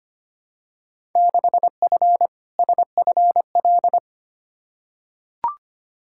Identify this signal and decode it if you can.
Morse code, heard as a keyed tone, about 25 wpm, 700 Hz. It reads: 6F SFL
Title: Field Day Exchanges Encore - 25wpm